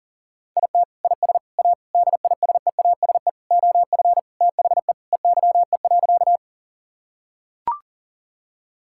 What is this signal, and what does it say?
Morse code, heard as a keyed tone, about 40 wpm, 700 Hz. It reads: IT IS A DISEASE OF THE EYE.
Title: Sentences from Top 1000 Words - Review Part 6 - 40wpm